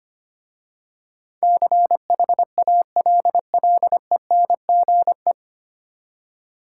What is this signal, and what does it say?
Morse code, heard as a keyed tone, about 25 wpm, 700 Hz. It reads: CHALLENGE
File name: practice / 25wpm / 9 Letter Words - First 3 Letters Given - 25wpm